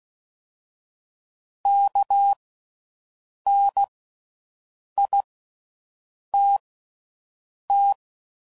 Morse code, heard as a keyed tone, about 16 wpm, spaced as for 6 wpm, 800 Hz. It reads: KNITT
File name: Slow With Faster Ings